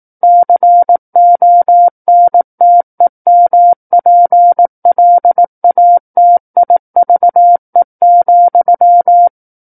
Morse code, heard as a keyed tone, about 18 wpm, 700 Hz. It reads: CONTEMPLATIVE,
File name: 18 WPM